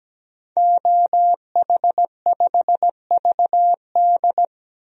Morse code, heard as a keyed tone, about 17 wpm, 700 Hz. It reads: OH5VD
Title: DXCC-100 Call Signs - 5 Characters - 17wpm